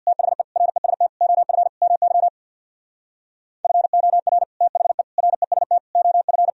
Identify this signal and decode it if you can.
Morse code, heard as a keyed tone, about 59 wpm, 700 Hz. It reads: THE REST OF MY FOR THE REST OF
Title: Speed-Racing - Sets of 5 Words 50wpm